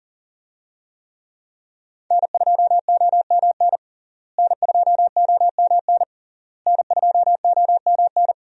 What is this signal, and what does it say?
Morse code, heard as a keyed tone, about 40 wpm, 700 Hz. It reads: N2OMN N2OMN N2OMN
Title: USA Call Signs - 5 Characters - Repeated 3x - 40wpm